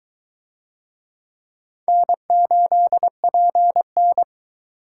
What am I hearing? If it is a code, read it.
Morse code, heard as a keyed tone, about 22 wpm, 700 Hz. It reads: N8PN